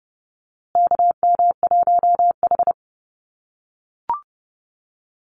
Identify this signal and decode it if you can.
Morse code, heard as a keyed tone, about 30 wpm, 700 Hz. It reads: KM1H
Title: USA Call Signs Repeated 3x 30wpm